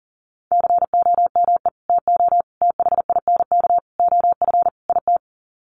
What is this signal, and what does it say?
Morse code, heard as a keyed tone, about 40 wpm, 700 Hz. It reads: COME TO THINK OF IT
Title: Sets of 5 Words Encore 40wpm